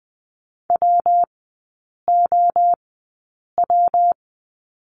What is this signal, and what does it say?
Morse code, heard as a keyed tone, about 20 wpm, spaced as for 8 wpm, 700 Hz. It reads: WOW
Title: 121 Lesson 20e - Number 3 - Three Characters 20wpm